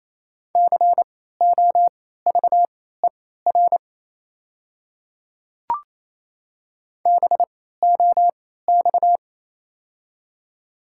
Morse code, heard as a keyed tone, about 28 wpm, spaced as for 16 wpm, 700 Hz. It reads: COVER BOX
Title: 3x Character Spacing - Top 500 Words - 28wpm